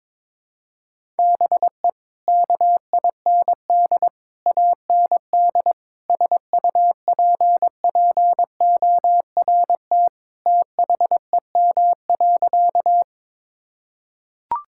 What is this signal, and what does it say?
Morse code, heard as a keyed tone, about 22 wpm, 700 Hz. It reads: BE KIND AND SUPPORT THEM.
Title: Sentences from Top 1000 Words - Review Part 5 - 22wpm